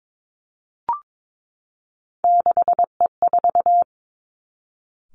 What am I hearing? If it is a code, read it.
Morse code, heard as a keyed tone, about 22 wpm, 700 Hz. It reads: 6E4
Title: Three Challenging Letter-Number Combinations 22wpm